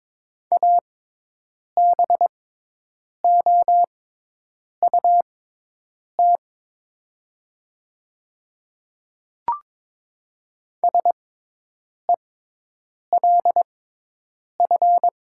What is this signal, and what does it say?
Morse code, heard as a keyed tone, about 22 wpm, spaced as for 8 wpm, 700 Hz. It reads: ABOUT SELF
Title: 6x Character Spacing - Top 500 Words - 22wpm